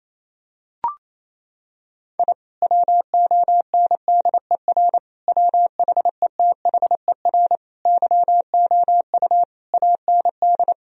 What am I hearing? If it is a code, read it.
Morse code, heard as a keyed tone, about 28 wpm, 700 Hz. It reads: I WONDER WHETHER YOU AND
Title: Sentences from Top 1000 Words - Review Part 3 - 28wpm